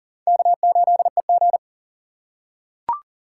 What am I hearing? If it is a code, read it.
Morse code, heard as a keyed tone, about 40 wpm, 700 Hz. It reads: K8EG
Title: USA Call Signs - 4 Characters - 6x Character Spacing - 40wpm